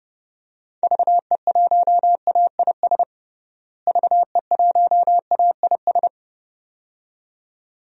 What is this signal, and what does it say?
Morse code, heard as a keyed tone, about 30 wpm, 700 Hz. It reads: VE1AIS VE1AIS
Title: DXCC-100 Call Signs - 6 Characters - Repeated 2x - 30wpm